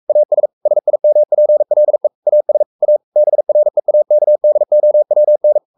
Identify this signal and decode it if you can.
Morse code, heard as a keyed tone, about 44 wpm, 600 Hz. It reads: AS SIMPLE AS A BREAKDOWN
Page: Romanen Never af Ken Follet på engelsk.